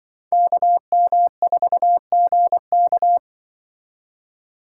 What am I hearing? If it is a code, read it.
Morse code, heard as a keyed tone, about 25 wpm, 700 Hz. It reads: KM4GK
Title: Speed Racing - USA Call Signs 20wpm